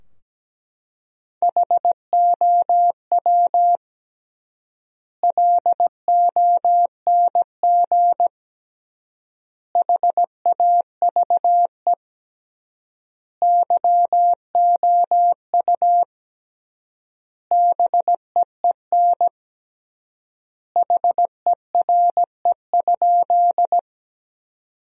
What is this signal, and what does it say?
Morse code, heard as a keyed tone, about 17 wpm, 700 Hz. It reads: HOW LONG HAVE YOU BEEN HERE?
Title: 3x Word-Spacing - Sentences from Top 200 Words - 17wpm